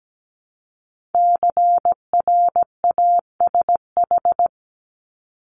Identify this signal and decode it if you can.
Morse code, heard as a keyed tone, about 17 wpm, 700 Hz. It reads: CRASH
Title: Multi-letter Phonemes 17wpm